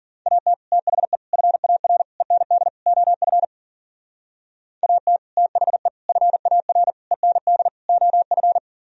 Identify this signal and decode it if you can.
Morse code, heard as a keyed tone, about 52 wpm, 700 Hz. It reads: AT THE FAR END OF AT THE FAR END OF